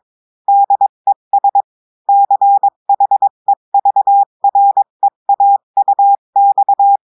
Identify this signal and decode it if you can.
Morse code, heard as a keyed tone, about 22 wpm, 800 Hz. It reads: DES CHEVREAUX
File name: cours-du-mercredi / audios / cours-853-du-2-mai-2024-etxte chiffre et texte clair-a-20-wpm